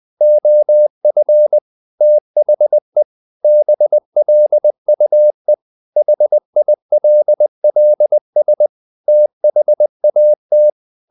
Morse code, heard as a keyed tone, about 20 wpm, 600 Hz. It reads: OF THE BLUE HILLS THAT
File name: the old man and the sea / the old man and the sea part 1 20wpm